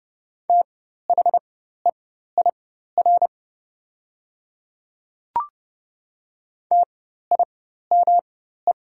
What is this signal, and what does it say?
Morse code, heard as a keyed tone, about 30 wpm, spaced as for 14 wpm, 700 Hz. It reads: THEIR TIME
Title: 4x Character Spacing - Top 200 Words - 30wpm